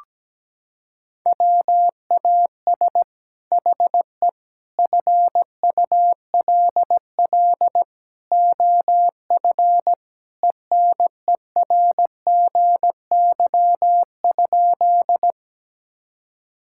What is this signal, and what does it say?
Morse code, heard as a keyed tone, about 17 wpm, 700 Hz. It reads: WAS HE FULL OF ENERGY?
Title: Sentences from Top 1000 Words - Review Part 6 - 17wpm